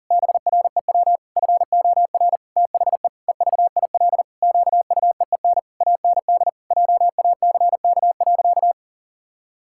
Morse code, heard as a keyed tone, about 40 wpm, 700 Hz. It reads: BREW FOR THE EVIL QUEEN AND JACK.
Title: Pangram Challenge v2 - 40wpm